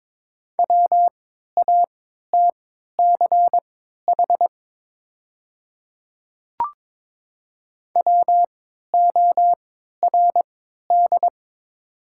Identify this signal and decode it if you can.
Morse code, heard as a keyed tone, about 22 wpm, spaced as for 13 wpm, 700 Hz. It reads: WATCH WORD
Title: 3x Character Spacing - Top 500 Words - 22wpm